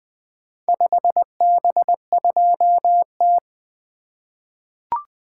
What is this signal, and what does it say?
Morse code, heard as a keyed tone, about 20 wpm, 700 Hz. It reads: 5B2T
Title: Four Letter-Number Combinations - 20wpm